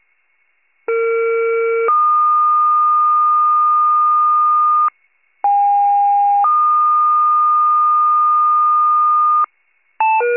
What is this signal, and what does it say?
Two-tone page, heard as a one-second tone, then a three-second tone, about 460 then 1220 Hz, about 810 then 1220 Hz. It is heard from 0.9 to 9.4 s.
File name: wcalert